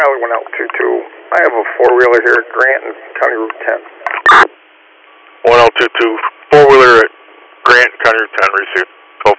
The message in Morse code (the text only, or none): none